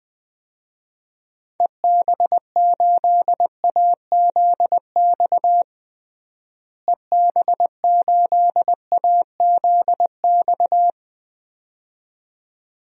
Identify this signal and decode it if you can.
Morse code, heard as a keyed tone, about 21 wpm, 700 Hz. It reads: EB8AZX EB8AZX
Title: DXCC-100 Call Signs - 6 Characters - Repeated 2x - 20wpm